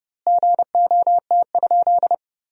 Morse code, heard as a keyed tone, about 31 wpm, 700 Hz. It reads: GOT?
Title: 2x Word-Spacing - Sentences from Top 300 Words - 30wpm